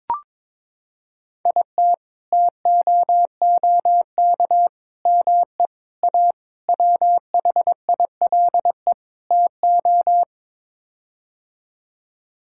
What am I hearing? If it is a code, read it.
Morse code, heard as a keyed tone, about 21 wpm, 700 Hz. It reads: IT TOOK ME A WHILE TO
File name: practice / 22wpm / Sets of 6 Words - 22wpm